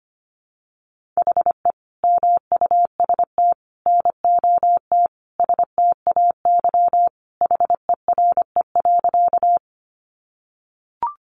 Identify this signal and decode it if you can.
Morse code, heard as a keyed tone, about 25 wpm, 700 Hz. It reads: HE MUST NOT STAY HERE.
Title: Sentences from Top 600 Words 25wpm